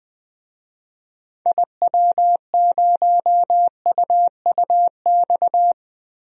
Morse code, heard as a keyed tone, about 20 wpm, 700 Hz. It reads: IW0UUX
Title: DXCC-100 Call Signs - 6 Characters - Repeated 2x - 20wpm